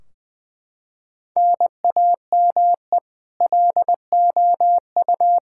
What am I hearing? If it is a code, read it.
Morse code, heard as a keyed tone, about 20 wpm, 700 Hz. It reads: NAME LOU
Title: Practice QSO - Exchange 1 - Beginning - 20wpm